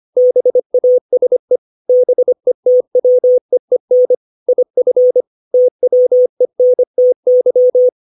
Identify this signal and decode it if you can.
Morse code, heard as a keyed tone, about 25 wpm, 500 Hz. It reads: BASE BETWEEN IF TWENTY
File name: 632 words / 632 Words 25WPM